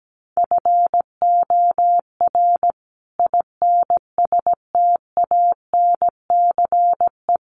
Morse code, heard as a keyed tone, about 17 wpm, 700 Hz. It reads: FOR INSTANCE
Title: Sentences from Top 800 Words 17wpm